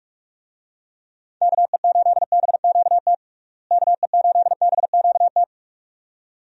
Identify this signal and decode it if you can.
Morse code, heard as a keyed tone, about 46 wpm, 700 Hz. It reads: KE8BQT KE8BQT
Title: USA Call Signs - 6 Characters - Repeated 2x - 45wpm